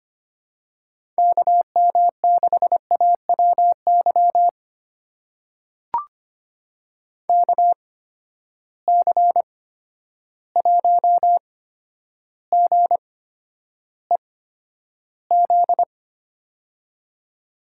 Morse code, heard as a keyed tone, about 25 wpm, spaced as for 11 wpm, 700 Hz. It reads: KM6AWY KC1GEZ
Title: USA Call Signs - 6 Characters - 8x Character Spacing - 25wpm